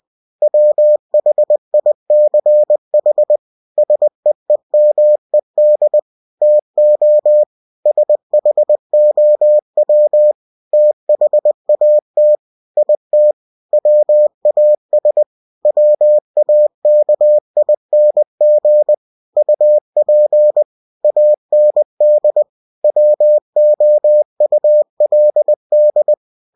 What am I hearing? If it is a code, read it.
Morse code, heard as a keyed tone, about 20 wpm, 600 Hz. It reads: WHICH SEEMED TO SHOW THAT IT WAS WAKING UP AND WOULD
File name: cw-books / winnie-the-pooh / winnie-20wpm-page-0040